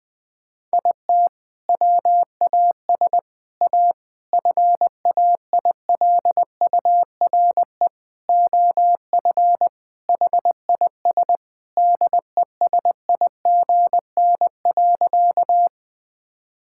Morse code, heard as a keyed tone, about 21 wpm, 700 Hz. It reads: IT WAS A FAILURE OF HIS DESIGN.
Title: Sentences from Top 1000 Words - Review Part 6 - 20wpm